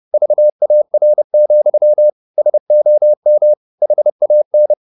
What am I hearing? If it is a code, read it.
Morse code, heard as a keyed tone, about 30 wpm, 600 Hz. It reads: VAR, SOM HAN
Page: Ferie 30 wpm